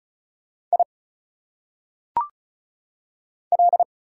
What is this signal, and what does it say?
Morse code, heard as a keyed tone, about 34 wpm, 700 Hz. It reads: I L
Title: Single Letters - Phonetic Pronunciation 35wpm